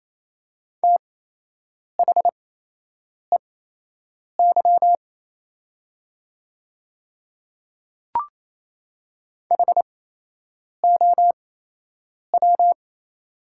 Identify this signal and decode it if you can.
Morse code, heard as a keyed tone, about 28 wpm, spaced as for 8 wpm, 700 Hz. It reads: THEY HOW
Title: 8x Character Spacing - Top 100 Words - 28wpm